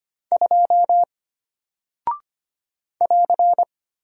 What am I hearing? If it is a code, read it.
Morse code, heard as a keyed tone, about 25 wpm, 700 Hz. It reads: 2 <AR>
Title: Single Character (Letter:Number:Punctuation:Prosign) Alt 25wpm